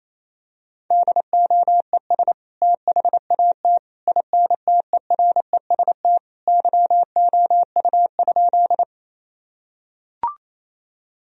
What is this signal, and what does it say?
Morse code, heard as a keyed tone, about 28 wpm, 700 Hz. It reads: DOES THAT INTEREST YOU?
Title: Sentences from Top 1000 Words - Review Part 2 - 28wpm